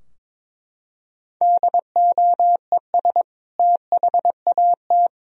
Morse code, heard as a keyed tone, about 22 wpm, 700 Hz. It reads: DOES THAT
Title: Sentences from Top 1000 Words - Review Part 2 - 22wpm